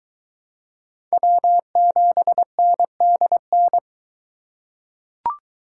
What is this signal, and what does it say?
Morse code, heard as a keyed tone, about 21 wpm, 700 Hz. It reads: W7NDN